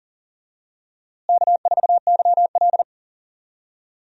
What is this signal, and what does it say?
Morse code, heard as a keyed tone, about 40 wpm, 700 Hz. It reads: K4YL
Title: USA Call Signs - 4 Characters - Repeated 2x - 40wpm